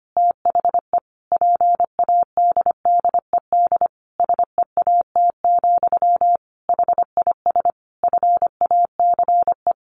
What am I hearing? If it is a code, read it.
Morse code, heard as a keyed tone, about 25 wpm, 700 Hz. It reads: THE PADDED SEAT, HIS FACE
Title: A Sound Of Thunder 25wpm